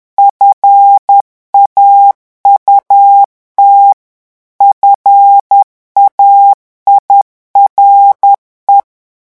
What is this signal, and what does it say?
Morse code, heard as a keyed tone, about 10 wpm, 800 Hz. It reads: FAUT FAIRE
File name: revision9 citations humoristiques vitesse 10 mots minute